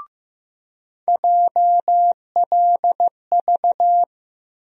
Morse code, heard as a keyed tone, about 15 wpm, 700 Hz. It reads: JLV
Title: USA Call Sign Postfixes - 3 Letters 15wpm